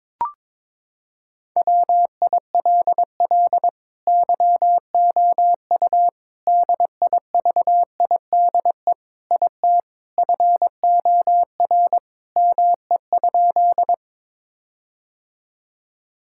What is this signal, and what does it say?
Morse code, heard as a keyed tone, about 22 wpm, 700 Hz. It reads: WILL YOU DIVIDE IT FOR ME?
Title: Sentences from Top 1600 Words 22wpm